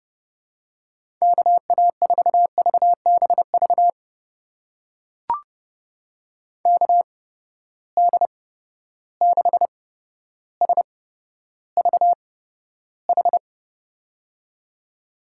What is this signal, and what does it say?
Morse code, heard as a keyed tone, about 30 wpm, spaced as for 13 wpm, 700 Hz. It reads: KA4VBV KD6SVH
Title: USA Call Signs - 6 Characters - 8x Character Spacing - 30wpm